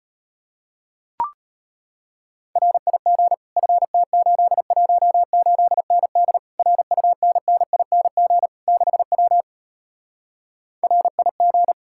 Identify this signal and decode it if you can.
Morse code, heard as a keyed tone, about 33 wpm, 700 Hz. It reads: RIG FT818ND RUNNING 6W RIG